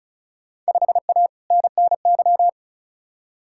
Morse code, heard as a keyed tone, about 36 wpm, 700 Hz. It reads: 5A NNY
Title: Field Day Exchanges - 35wpm